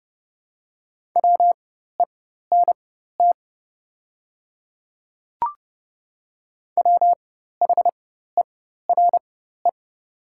Morse code, heard as a keyed tone, about 30 wpm, spaced as for 14 wpm, 700 Hz. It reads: WENT WHERE
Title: 4x Character Spacing - Top 200 Words - 30wpm